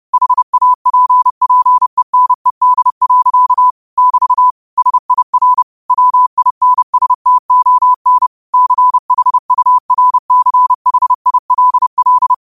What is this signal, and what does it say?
Morse code, heard as a keyed tone, about 30 wpm, 1000 Hz. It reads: HAPPENED. <BT> SIR WINSTON CHURCHILL
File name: qotdfiles / 30wpm